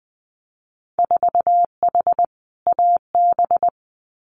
Morse code, heard as a keyed tone, about 20 wpm, 700 Hz. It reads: 4H AB
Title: Winter Field Day Exchanges Encore 20wpm